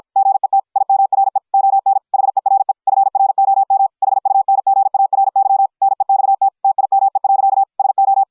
Morse code, heard as a keyed tone, about 53 wpm, 750 Hz. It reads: GET IMRE ON HERE FROM HUNGARY NEXT TIME. SO